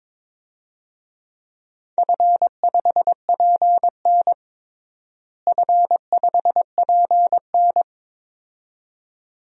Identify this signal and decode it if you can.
Morse code, heard as a keyed tone, about 22 wpm, 700 Hz. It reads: F5PN F5PN
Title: DXCC-100 Call Signs - 4 Characters - Repeated 2x - 22wpm